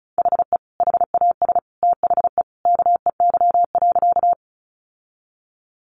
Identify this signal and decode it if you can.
Morse code, heard as a keyed tone, about 34 wpm, 700 Hz. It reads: HE HAS THE KEY.
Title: Sentences from Top 600 Words 35wpm